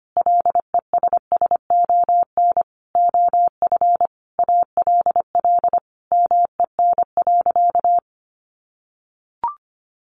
Morse code, heard as a keyed tone, about 25 wpm, 700 Hz. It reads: LESSON OF ALL MEN.
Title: Sentences from Top 1200 Words 25wpm